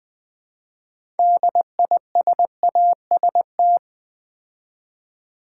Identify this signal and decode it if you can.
Morse code, heard as a keyed tone, about 20 wpm, 700 Hz. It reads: DISAST
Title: Build A Word - 2 Letters At A Time - 8-Letter Words - 20wpm